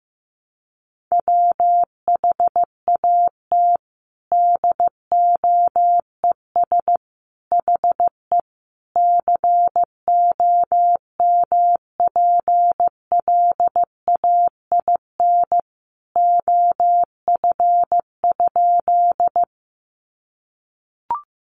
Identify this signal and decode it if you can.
Morse code, heard as a keyed tone, about 15 wpm, 700 Hz. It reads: WHAT DOES HE COMPLAIN OF?
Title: Sentences from Top 1300 Words 15wpm